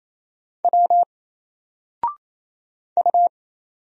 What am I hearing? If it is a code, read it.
Morse code, heard as a keyed tone, about 27 wpm, 700 Hz. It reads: W U
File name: Single-Letter-Number-28wpm